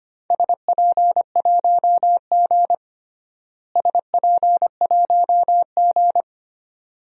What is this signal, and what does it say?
Morse code, heard as a keyed tone, about 25 wpm, 700 Hz. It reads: SP1G SP1G
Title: DXCC-100 Call Signs - 4 Characters - Repeated 2x - 25wpm